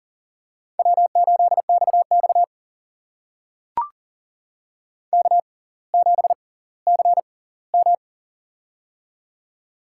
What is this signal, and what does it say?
Morse code, heard as a keyed tone, about 38 wpm, spaced as for 20 wpm, 700 Hz. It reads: W8XX K7CM
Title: USA Call Signs - 4 Characters - 6x Character Spacing - 40wpm